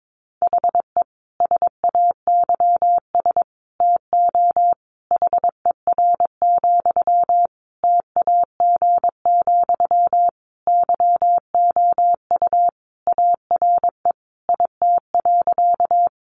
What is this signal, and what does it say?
Morse code, heard as a keyed tone, about 22 wpm, 700 Hz. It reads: HE SAYS TO HER, TAG, YOU ARE IT.
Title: Additional Easy Phrases 22wpm